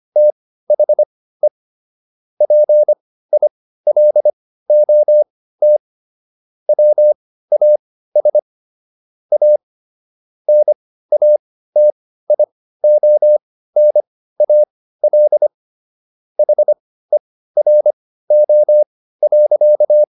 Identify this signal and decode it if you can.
Morse code, heard as a keyed tone, about 25 wpm, spaced as for 15 wpm, 600 Hz. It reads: THE PILOT WAS A NATIONAL HERO.